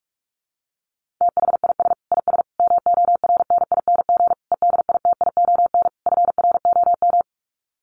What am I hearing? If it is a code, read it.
Morse code, heard as a keyed tone, about 44 wpm, 700 Hz. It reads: THIS IS MORNING EDITION FROM
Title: Sets of 5 Words Encore2 45wpm